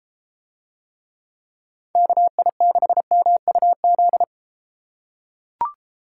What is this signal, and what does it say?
Morse code, heard as a keyed tone, about 31 wpm, 700 Hz. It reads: KI6MUZ